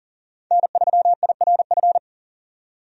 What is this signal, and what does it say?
Morse code, heard as a keyed tone, about 37 wpm, 700 Hz. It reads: N3IRF
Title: USA Call Signs - 5 Characters - 40wpm